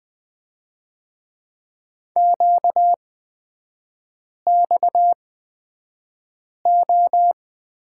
Morse code, heard as a keyed tone, about 20 wpm, spaced as for 5 wpm, 700 Hz. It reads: QXO
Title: 236 Lesson 38e - Letter q - Three Characters 20wpm